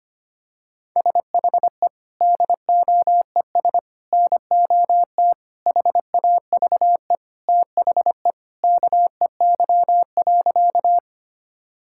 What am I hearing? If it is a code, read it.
Morse code, heard as a keyed tone, about 25 wpm, 700 Hz. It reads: SHE DOES NOT HAVE THE KEY.
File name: practice / 25wpm / Sentences with 3-Word Phrases - 25wpm